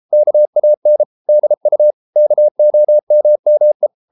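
Morse code, heard as a keyed tone, about 33 wpm, 600 Hz. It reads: KAN DU KOMME
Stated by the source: Morse code at 33 wpm, QRQ.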